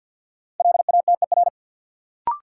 Morse code, heard as a keyed tone, about 48 wpm, 700 Hz. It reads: LATER
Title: Sentences from Top 700 Words - New Word Drill - 50wpm